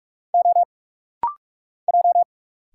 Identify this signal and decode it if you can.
Morse code, heard as a keyed tone, about 46 wpm, 700 Hz. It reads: O J
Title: Single Letters - Rapid-Fire 45wpm